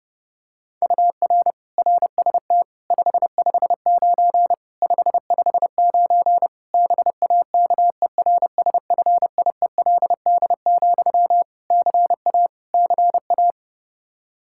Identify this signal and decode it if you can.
Morse code, heard as a keyed tone, about 30 wpm, 700 Hz. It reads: UR RST 559 559 BAKERSFIELD, CA CA
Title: K1USN SST QSO Practice 30wpm